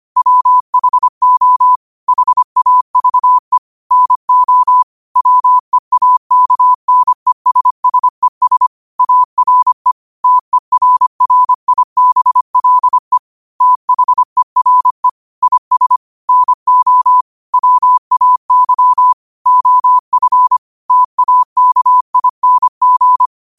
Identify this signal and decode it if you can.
Morse code, heard as a keyed tone, about 25 wpm, 1000 Hz. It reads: WHO HAVE NO WEAKNESSES ARE TERRIBLE THERE IS NO WAY OF TAKING
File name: qotdfiles / 25wpm